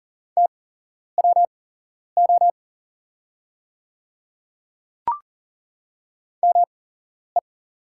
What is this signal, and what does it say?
Morse code, heard as a keyed tone, about 41 wpm, spaced as for 11 wpm, 700 Hz. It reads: TWO ME
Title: 8x Character Spacing - Top 300 Words - 40wpm